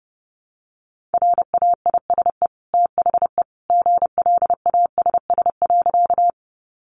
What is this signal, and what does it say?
Morse code, heard as a keyed tone, about 30 wpm, 700 Hz. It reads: RAISE THE GLASS.
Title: Sentences from Top 1400 Words 30wpm